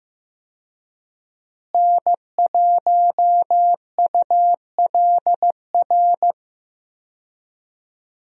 Morse code, heard as a keyed tone, about 15 wpm, 700 Hz. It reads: N1ULR
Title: USA Call Signs - 5 Characters - Repeated 2x - 15wpm